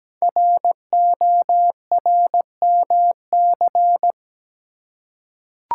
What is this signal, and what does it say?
Morse code, heard as a keyed tone, about 17 wpm, 700 Hz. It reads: RORMC
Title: Code Groups - 5 Letters - 17wpm